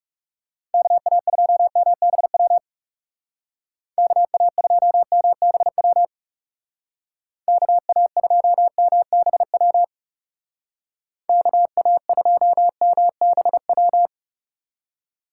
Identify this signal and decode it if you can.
Morse code, heard as a keyed tone, about 36 wpm, 700 Hz. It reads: KA2MBW KA2MBW KA2MBW KA2MBW
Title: Speed Racing - USA 2x3 Call Signs 30wpm